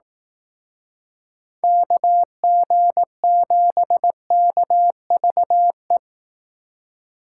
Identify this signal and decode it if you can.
Morse code, heard as a keyed tone, about 18 wpm, 700 Hz. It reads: KG7KVE
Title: Speed Racing - USA 2x3 Call Signs 15wpm